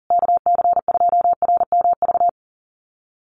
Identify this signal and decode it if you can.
Morse code, heard as a keyed tone, about 40 wpm, 700 Hz. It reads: KC2RMV
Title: USA Call Signs Repeated 3x 40wpm